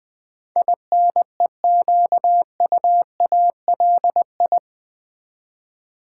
Morse code, heard as a keyed tone, about 20 wpm, 700 Hz. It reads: INEQUALI
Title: Build A Word - 4 Letters At A Time - 10-Letter Words - 20wpm